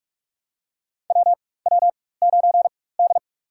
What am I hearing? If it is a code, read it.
Morse code, heard as a keyed tone, about 47 wpm, spaced as for 21 wpm, 700 Hz. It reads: WW9D
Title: USA Call Signs - 4 Characters - 4x Character Spacing - 45wpm